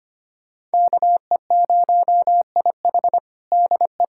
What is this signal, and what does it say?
Morse code, heard as a keyed tone, about 25 wpm, 700 Hz. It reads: KE0IH DE
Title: USA Callsign DE Callsign 25wpm